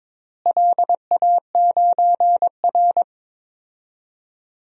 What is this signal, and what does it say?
Morse code, heard as a keyed tone, about 22 wpm, 700 Hz. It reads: LA9R
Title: DXCC-100 Call Signs - 4 Characters - Repeated 2x - 22wpm